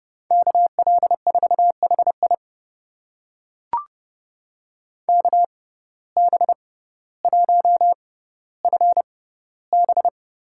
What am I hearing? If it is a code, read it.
Morse code, heard as a keyed tone, about 29 wpm, spaced as for 15 wpm, 700 Hz. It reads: KL4HI KB1FB
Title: USA Call Signs - 5 Characters - 6x Character Spacing - 30wpm